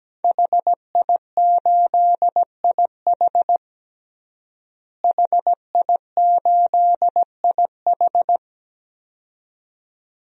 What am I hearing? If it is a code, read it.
Morse code, heard as a keyed tone, about 17 wpm, 700 Hz. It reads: HI8IH HI8IH
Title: DXCC-100 Call Signs - 5 Characters - Repeated 2x - 17wpm